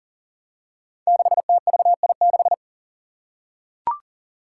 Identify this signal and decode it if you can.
Morse code, heard as a keyed tone, about 40 wpm, 700 Hz. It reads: 6TVI6
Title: Five Challenging Letter-Number Combinations 40wpm